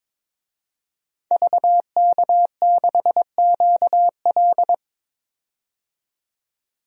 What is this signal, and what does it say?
Morse code, heard as a keyed tone, about 22 wpm, 700 Hz. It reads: VK6QL
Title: DXCC-100 Call Signs - 5 Characters - 22wpm